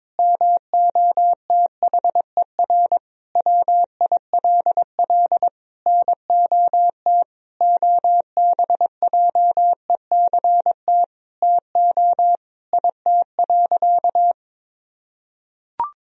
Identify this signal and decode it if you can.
Morse code, heard as a keyed tone, about 22 wpm, 700 Hz. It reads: MOTHER WILL NOT OBJECT TO IT.
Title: Sentences from Top 1000 Words - Review Part 3 - 22wpm